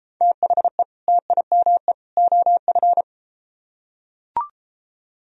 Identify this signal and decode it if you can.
Morse code, heard as a keyed tone, about 33 wpm, 700 Hz. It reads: THE TIME OF